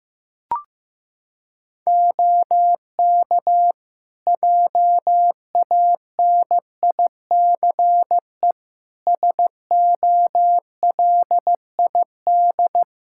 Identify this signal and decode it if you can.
Morse code, heard as a keyed tone, about 16 wpm, 700 Hz. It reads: OK JANICE SOLID
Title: K1USN SST QSO Practice 15wpm